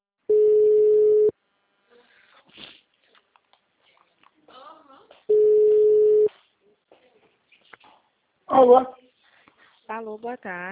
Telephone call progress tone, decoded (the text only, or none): ringback tone